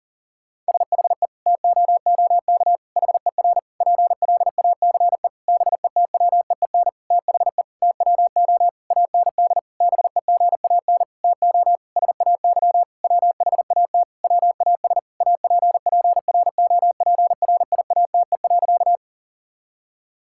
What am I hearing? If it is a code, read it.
Morse code, heard as a keyed tone, about 40 wpm, 700 Hz. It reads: SHE TOOK HER PLACE BETWEEN THE TWO AND BEGAN TO SAY WHAT WAS APPROPRIATE.
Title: Sentences from Top 1000 Words - Review Part 5 - 40wpm